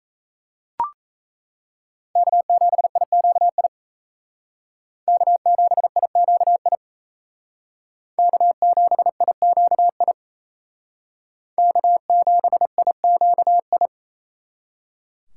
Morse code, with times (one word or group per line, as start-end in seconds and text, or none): 2.2-3.7 K7IQI
5.1-6.8 K7IQI
8.2-10.1 K7IQI
11.6-13.9 K7IQI